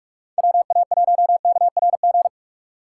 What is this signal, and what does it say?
Morse code, heard as a keyed tone, about 46 wpm, 700 Hz. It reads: WA1KRG
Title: Speed Racing - USA 2x3 Call Signs 45wpm